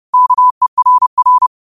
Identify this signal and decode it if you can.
Morse code, heard as a keyed tone, about 30 wpm, 1000 Hz. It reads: KERR